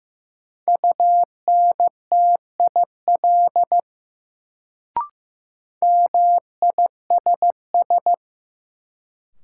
Morse code, heard as a keyed tone, about 16 wpm, 700 Hz. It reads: UNTIL MISS
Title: Sentences from Top 500 Words - New Word Drill - 15wpm